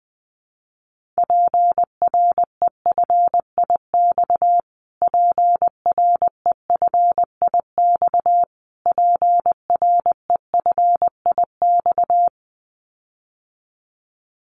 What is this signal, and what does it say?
Morse code, heard as a keyed tone, about 19 wpm, 700 Hz. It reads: PREFIX PREFIX PREFIX
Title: Callsign Prefixes for Top 25 DXCC - Repeated 3x - 20wpm